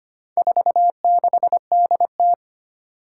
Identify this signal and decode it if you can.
Morse code, heard as a keyed tone, about 25 wpm, 700 Hz. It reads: 46DT
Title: Four Challenging Letter-Number Combinations 25wpm